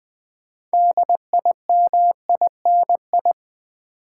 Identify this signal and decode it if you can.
Morse code, heard as a keyed tone, about 20 wpm, 700 Hz. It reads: DIMINI
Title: Build A Word - 2 Letters At A Time - 8-Letter Words - 20wpm